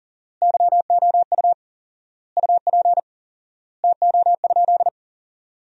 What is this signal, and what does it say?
Morse code, heard as a keyed tone, about 40 wpm, 700 Hz. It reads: YOU UP TO?
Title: 4x Word-Spacing - Sentences from Top 100 Words - 40wpm